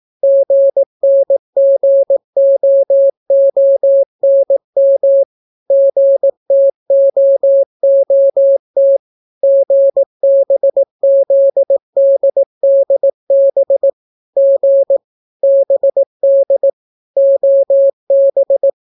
Morse code, heard as a keyed tone, about 18 wpm, 550 Hz. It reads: GNGOONM GTOOT GBZDDB G BD OB